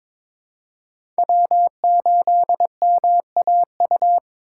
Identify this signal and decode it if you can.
Morse code, heard as a keyed tone, about 22 wpm, 700 Hz. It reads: W8MAU
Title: USA Call Signs - 5 Characters - 3x Character Spacing - 22wpm